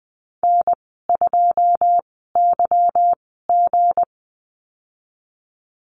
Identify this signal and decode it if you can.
Morse code, heard as a keyed tone, about 20 wpm, 700 Hz. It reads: N 2 Y G
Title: 200 Lesson 32g - Letter k - Call Signs 20wpm